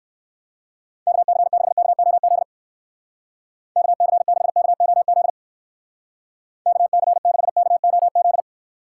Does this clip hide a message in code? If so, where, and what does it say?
Yes: Morse code from 1.1 to 8.4 s, heard as a keyed tone, about 62 wpm, 700 Hz. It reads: KC6KOZ KC6KOZ KC6KOZ